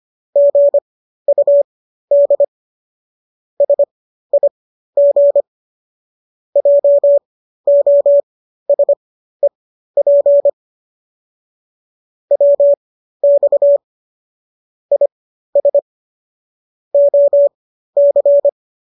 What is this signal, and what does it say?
Morse code, heard as a keyed tone, about 25 wpm, spaced as for 13 wpm, 570 Hz. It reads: GUD SIG JOSEP WX IS OC
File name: qso214 13